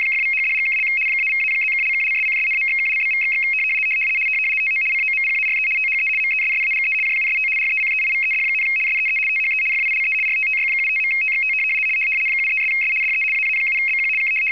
RTTY, about 75 baud. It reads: !=#$&() -=-+()()':',./()?=// TO COMPARE THE MESSAGE JUST RECEIVED, SEE THE FILE RTTY75.TXT 73'S DE AA7BQ K
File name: w ry425 / rtty75